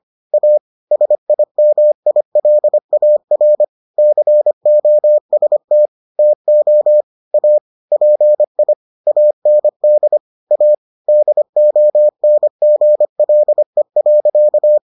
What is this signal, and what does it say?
Morse code, heard as a keyed tone, about 25 wpm, 600 Hz. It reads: A SIMILAR COST TO A PI AND A DONGLE.